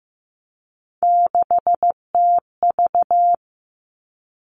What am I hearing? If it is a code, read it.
Morse code, heard as a keyed tone, about 15 wpm, 700 Hz. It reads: 6TV
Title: Three Challenging Letter-Number Combinations 15wpm